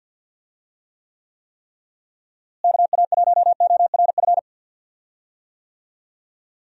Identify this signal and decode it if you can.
Morse code, heard as a keyed tone, about 52 wpm, 700 Hz. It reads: KA1ORF
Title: USA Call Signs - 6 Characters - Repeated 3x - 50wpm